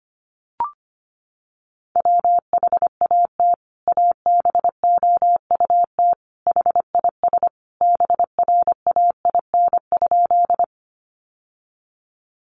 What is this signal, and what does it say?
Morse code, heard as a keyed tone, about 25 wpm, 700 Hz. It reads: WHAT ABOUT HIS BRAIN?
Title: Sentences from Top 1100 Words 25wpm